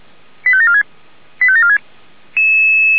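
Five-tone selective call (ZVEI), heard as quick runs of five tones, tones of about 70 ms. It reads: 85746 85746
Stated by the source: ZVEI five-tone selective call received by SDR.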